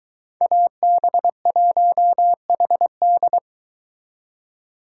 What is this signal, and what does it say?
Morse code, heard as a keyed tone, about 22 wpm, 700 Hz. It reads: AB1HD